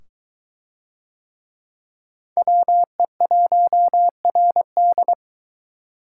Morse code, heard as a keyed tone, about 22 wpm, 700 Hz. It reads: WE1RD